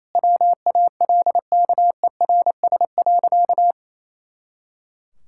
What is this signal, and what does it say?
Morse code, heard as a keyed tone, about 28 wpm, 700 Hz. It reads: WALKERS.
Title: Pangram Challenge v2 - 28wpm